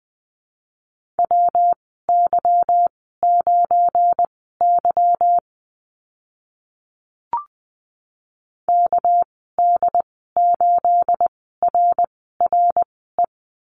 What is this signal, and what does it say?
Morse code, heard as a keyed tone, about 19 wpm, spaced as for 14 wpm, 700 Hz. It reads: WY9Y KD8RRE
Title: 219 Lesson 35g - Number 0 - Call Signs 20wpm